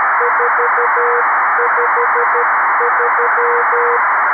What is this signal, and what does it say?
Morse code, heard as a keyed tone, about 12 wpm, 480 Hz, with noise. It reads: 453
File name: arissat / recordings / ARISSat 11Sept2011 1354utc 60frames